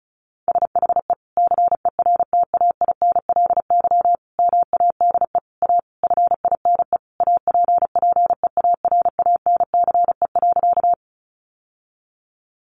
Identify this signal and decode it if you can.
Morse code, heard as a keyed tone, about 34 wpm, 700 Hz. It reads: SHE CERTAINLY MADE A FINE APPEARANCE.
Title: Sentences from Top 1100 Words 35wpm